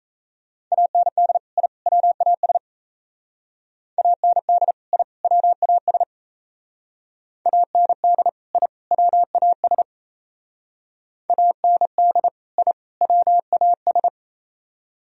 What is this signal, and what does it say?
Morse code, heard as a keyed tone, about 36 wpm, 700 Hz. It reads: AND I WAS AND I WAS AND I WAS AND I WAS